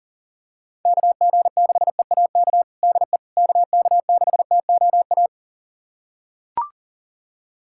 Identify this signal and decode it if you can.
Morse code, heard as a keyed tone, about 40 wpm, 700 Hz. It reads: KG6EAK DE KK6TOA
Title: USA Callsign DE Callsign - 6 Characters - 40wpm